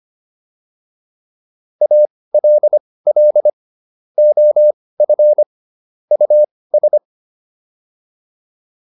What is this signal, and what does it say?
Morse code, heard as a keyed tone, about 25 wpm, spaced as for 18 wpm, 600 Hz. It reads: ALL OF US